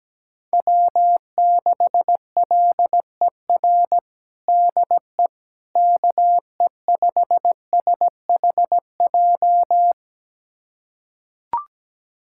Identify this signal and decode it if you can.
Morse code, heard as a keyed tone, about 17 wpm, 700 Hz. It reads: W6LER DE KE5SHJ
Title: USA Callsign DE Callsign 17wpm